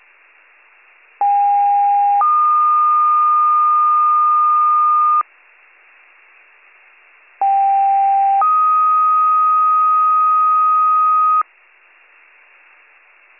Two-tone page, heard as a one-second tone, then a three-second tone, about 810 then 1220 Hz, about 790 then 1250 Hz. It is heard from 1.2 to 11.4 s.